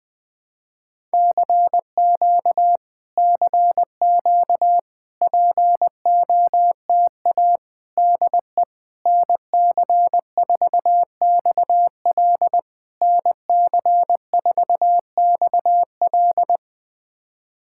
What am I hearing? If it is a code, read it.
Morse code, heard as a keyed tone, about 20 wpm, 700 Hz. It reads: CQ CQ POTA DE NC4XL NC4XL
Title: POTA QSO Practice - 20wpm